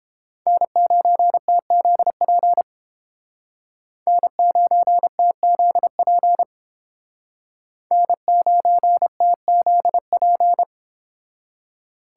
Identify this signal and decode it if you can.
Morse code, heard as a keyed tone, about 31 wpm, 700 Hz. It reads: N9TZP N9TZP N9TZP